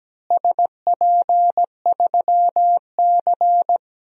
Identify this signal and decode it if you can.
Morse code, heard as a keyed tone, about 17 wpm, 700 Hz. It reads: SP3C
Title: DXCC-100 Call Signs - 4 Characters - Repeated 3x - 17wpm